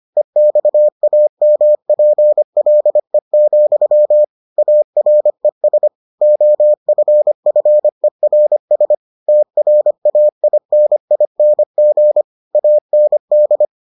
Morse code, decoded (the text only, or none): EXAMPLE, ARES OFFERS TRAINING AND